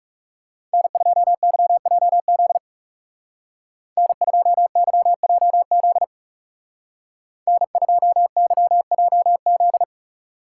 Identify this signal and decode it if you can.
Morse code, heard as a keyed tone, about 39 wpm, 700 Hz. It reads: N2YJZ N2YJZ N2YJZ